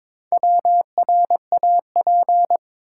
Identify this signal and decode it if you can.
Morse code, heard as a keyed tone, about 22 wpm, 700 Hz. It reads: WRAP
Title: Four Letter Words - 22wpm